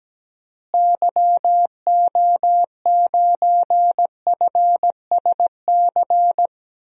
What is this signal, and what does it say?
Morse code, heard as a keyed tone, about 17 wpm, 700 Hz. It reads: YO9FSC
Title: DXCC-100 Call Signs - 6 Characters - Repeated 2x - 17wpm